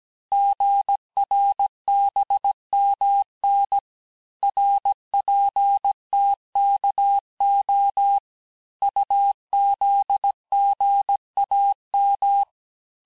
Morse code, heard as a keyed tone, about 17 wpm, 800 Hz. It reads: GRBMN RPTKO UZGAM
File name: cours-du-lundi files / corrige-cours-1581-texte-chiffre-vitesse-16-mpm